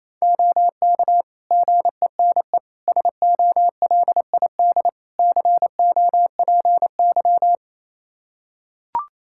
Morse code, heard as a keyed tone, about 28 wpm, 700 Hz. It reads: OK GENE SOLID COPY
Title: Practice QSO - Exchange 3 - More Info - 28wpm